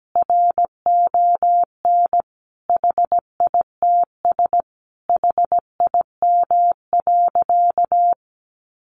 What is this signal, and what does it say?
Morse code, heard as a keyed tone, about 17 wpm, 700 Hz. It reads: RON HITS HIM.
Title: Additional Easy Phrases 17wpm